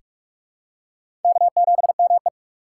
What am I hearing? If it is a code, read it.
Morse code, heard as a keyed tone, about 46 wpm, 700 Hz. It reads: K7ME
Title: USA Call Signs - 4 Characters - 2x Character Spacing - 45wpm